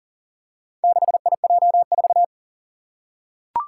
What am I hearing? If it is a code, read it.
Morse code, heard as a keyed tone, about 40 wpm, 700 Hz. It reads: 6IJ4
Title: Four Challenging Letter-Number Combinations 40wpm